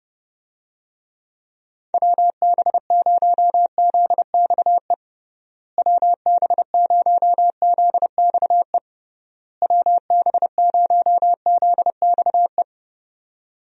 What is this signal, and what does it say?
Morse code, heard as a keyed tone, about 30 wpm, 700 Hz. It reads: WB0ZXE WB0ZXE WB0ZXE
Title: USA Call Signs - 6 Characters - Repeated 3x - 30wpm